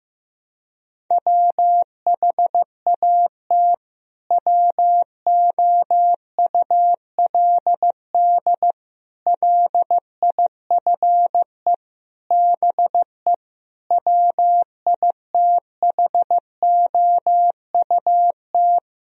Morse code, heard as a keyed tone, about 16 wpm, 700 Hz. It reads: WHAT WOULD LIFE BE WITHOUT
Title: Sentences from Top 1000 Words - Review Part 3 - 15wpm